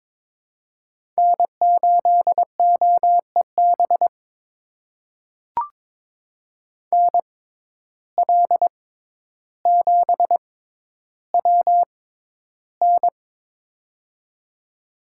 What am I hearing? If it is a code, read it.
Morse code, heard as a keyed tone, about 23 wpm, spaced as for 11 wpm, 700 Hz. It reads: N8OEB NL7WN
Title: USA Call Signs - 5 Characters - 6x Character Spacing - 22wpm